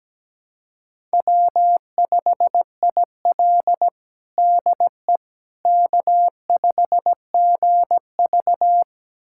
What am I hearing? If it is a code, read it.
Morse code, heard as a keyed tone, about 17 wpm, 700 Hz. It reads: W5IL DE K5GV
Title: USA Callsign DE Callsign - 4 Characters - 17wpm